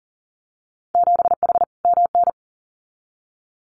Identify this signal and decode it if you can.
Morse code, heard as a keyed tone, about 40 wpm, 700 Hz. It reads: 7H MN
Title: Winter Field Day Exchanges Encore 40wpm